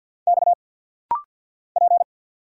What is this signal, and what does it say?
Morse code, heard as a keyed tone, about 49 wpm, 700 Hz. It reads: X P
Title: Single Letter-Number Alt - Rapid-Fire 50wpm